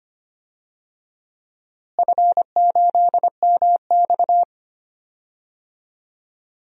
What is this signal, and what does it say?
Morse code, heard as a keyed tone, about 25 wpm, 700 Hz. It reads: F8MX
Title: DXCC-100 Call Signs - 4 Characters - Repeated 3x - 25wpm